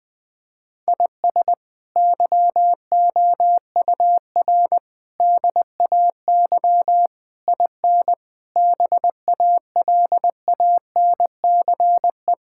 Morse code, heard as a keyed tone, about 21 wpm, 700 Hz. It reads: IS YOUR DAY IN BALANCE
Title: Sentences from Top 1000 Words - Review Part 6 - 20wpm